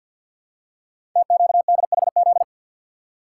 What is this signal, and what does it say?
Morse code, heard as a keyed tone, about 52 wpm, 700 Hz. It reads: TQDHZ
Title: Code Groups - 5 Letter:Numbers - 50wpm